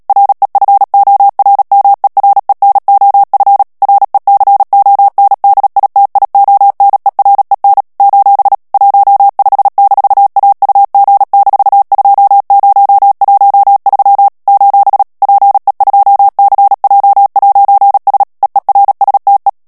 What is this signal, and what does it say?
Morse code, heard as a keyed tone, about 37 wpm, 800 Hz. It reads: REFORMEREN OF RECONDITIONEREN : 15-AUG-2013 : PE2CJ'S EERSTE
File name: morse mp3 / HOLLANDAIS texte clair 2 vitesse 35 mots minute